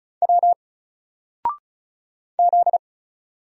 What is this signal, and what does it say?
Morse code, heard as a keyed tone, about 36 wpm, 700 Hz. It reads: W Z
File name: Single-Letter-Number-35wpm